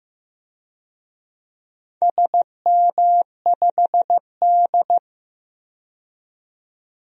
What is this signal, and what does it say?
Morse code, heard as a keyed tone, about 15 wpm, 700 Hz. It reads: SM5D
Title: DXCC-100 Call Signs - 4 Characters - 15wpm